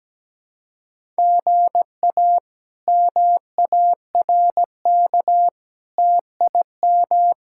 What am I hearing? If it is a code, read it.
Morse code, heard as a keyed tone, about 17 wpm, 700 Hz. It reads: GA MARK TIM
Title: K1USN SST QSO Practice - 17wpm